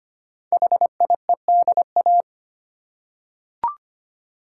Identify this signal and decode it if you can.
Morse code, heard as a keyed tone, about 25 wpm, 700 Hz. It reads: HIEDA
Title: Code Groups - 5 Letters - 25wpm